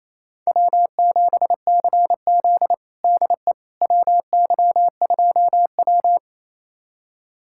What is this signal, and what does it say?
Morse code, heard as a keyed tone, about 28 wpm, 700 Hz. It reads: W7CZ DE WY2W
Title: USA Callsign DE Callsign - 4 Characters - 28wpm